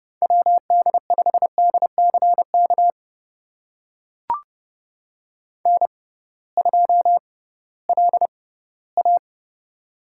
Morse code, heard as a keyed tone, about 30 wpm, spaced as for 18 wpm, 700 Hz. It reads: WD5DCK N2LA
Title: USA Call Signs - 6x Character Spacing - 30wpm